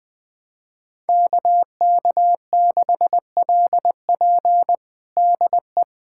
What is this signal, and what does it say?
Morse code, heard as a keyed tone, about 20 wpm, 700 Hz. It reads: KK6LP DE
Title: USA Callsign DE Callsign - 5 Characters - 20wpm